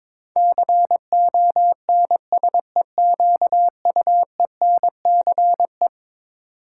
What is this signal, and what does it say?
Morse code, heard as a keyed tone, about 22 wpm, 700 Hz. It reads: CONSEQUENCE
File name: practice / 22wpm / Eleven Letter Words - 22wpm